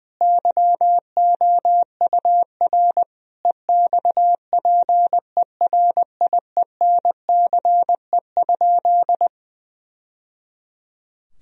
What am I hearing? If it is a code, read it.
Morse code, heard as a keyed tone, about 20 wpm, 700 Hz. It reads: YOUR EXPERIENCE?
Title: Sentences from Top 1000 Words - Review Part 4 - 20wpm